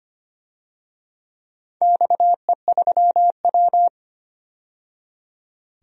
Morse code, heard as a keyed tone, about 25 wpm, 700 Hz. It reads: XE3W
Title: DXCC-100 Call Signs - 4 Characters - 25wpm